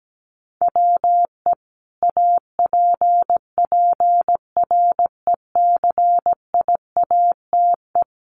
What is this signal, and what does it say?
Morse code, heard as a keyed tone, about 17 wpm, 700 Hz. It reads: WE APPRECIATE
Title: Sentences from Top 1300 Words 17wpm